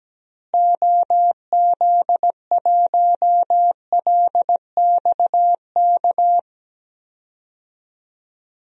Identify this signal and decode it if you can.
Morse code, heard as a keyed tone, about 17 wpm, 700 Hz. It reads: OZ1LXK
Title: DXCC-100 Call Signs - 6 Characters - Repeated 2x - 17wpm